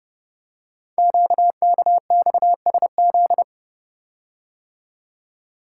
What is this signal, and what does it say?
Morse code, heard as a keyed tone, about 30 wpm, 690 Hz. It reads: QKXSZ
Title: Code Groups - 5 Letters - 30wpm